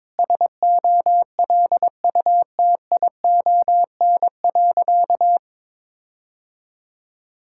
Morse code, heard as a keyed tone, about 22 wpm, 700 Hz. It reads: SOLUTION.
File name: practice / 22wpm / Sentences with 8-Word Phrases - 22wpm